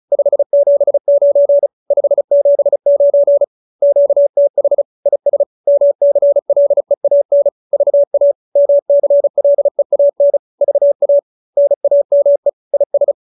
Morse code, heard as a keyed tone, about 34 wpm, 570 Hz. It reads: 579 579 QTH IS MCLEAN VA MCLEAN VA NAME IS